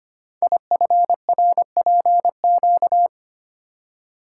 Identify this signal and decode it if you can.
Morse code, heard as a keyed tone, about 25 wpm, 700 Hz. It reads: IFRPQ
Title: Code Groups - 5 Letter:Numbers - 25wpm